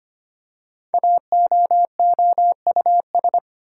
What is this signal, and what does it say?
Morse code, heard as a keyed tone, about 25 wpm, 700 Hz. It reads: AOOUS
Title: Code Groups - 5 Letters - 25wpm